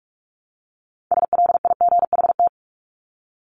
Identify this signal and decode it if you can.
Morse code, heard as a keyed tone, about 44 wpm, 700 Hz. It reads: SLIGHT
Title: LY Suffix - Top 100 Words with Base Words 45wpm